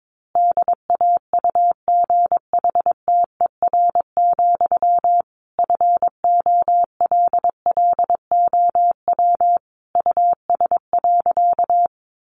Morse code, heard as a keyed tone, about 22 wpm, 700 Hz. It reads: DAUGHTER, FOLLOW US.
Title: Sentences from Top 1500 Words 22wpm